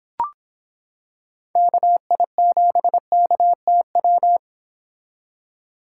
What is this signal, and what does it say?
Morse code, heard as a keyed tone, about 26 wpm, 700 Hz. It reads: KI7KTW